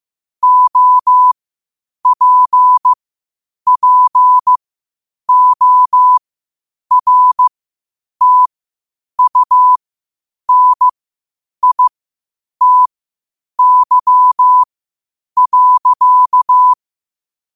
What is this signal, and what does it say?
Morse code, heard as a keyed tone, about 15 wpm, spaced as for 8 wpm, 1000 Hz. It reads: OPPORTUNITY.
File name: qotdfiles / 8wpm